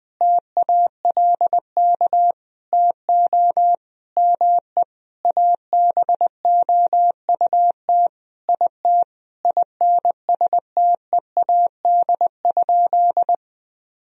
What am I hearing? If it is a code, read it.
Morse code, heard as a keyed tone, about 20 wpm, 700 Hz. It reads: TALK TO ME ABOUT IT INSTEAD?
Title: Sentences from Top 1000 Words - Review Part 4 - 20wpm